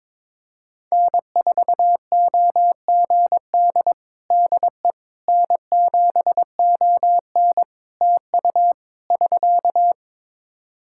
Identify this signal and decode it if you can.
Morse code, heard as a keyed tone, about 22 wpm, 700 Hz. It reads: N4OGD DE N7ON TU <SK>
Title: Practice QSO - Exchange 4 - Ending - 22wpm